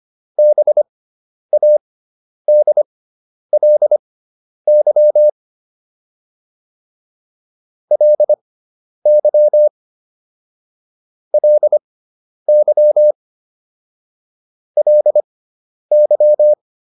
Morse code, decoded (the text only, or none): BADLY LY LY LY